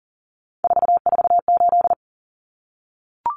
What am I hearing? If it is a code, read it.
Morse code, heard as a keyed tone, about 40 wpm, 700 Hz. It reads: 448
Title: Numbers - 3-Digit 40wpm